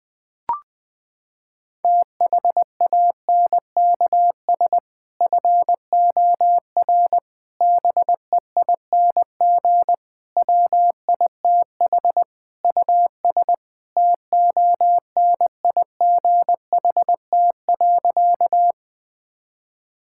Morse code, heard as a keyed tone, about 20 wpm, 700 Hz. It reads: THANKS FOR BEING WITH US TONIGHT.
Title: Sentences with 5-Word Phrases - 20wpm